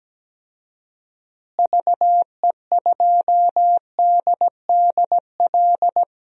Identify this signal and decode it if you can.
Morse code, heard as a keyed tone, about 17 wpm, 700 Hz. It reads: VE2DDL
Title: DXCC-100 Call Signs - 6 Characters - Repeated 2x - 17wpm